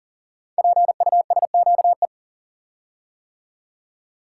Morse code, heard as a keyed tone, about 40 wpm, 700 Hz. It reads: PUSQE
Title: Code Groups - 5 Letters - 40wpm